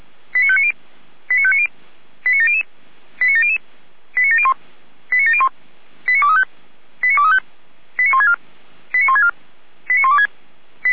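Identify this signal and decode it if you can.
Five-tone selective call (ZVEI), heard as quick runs of five tones, tones of about 70 ms. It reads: 89590 89590 89700 89700 89701 89701 89235 89235 89164 89164 89126
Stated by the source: Five-tone ZVEI selective call received on SDR.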